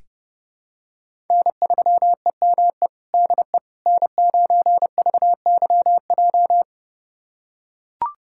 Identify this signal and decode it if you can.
Morse code, heard as a keyed tone, about 30 wpm, 700 Hz. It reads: N3EME DE N9VYJ
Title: USA Callsign DE Callsign - 5 Characters - 30wpm